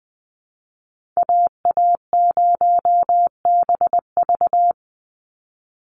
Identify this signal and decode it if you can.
Morse code, heard as a keyed tone, about 20 wpm, 700 Hz. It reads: AA0BV
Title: NA SOTA Activator Callsigns 20wpm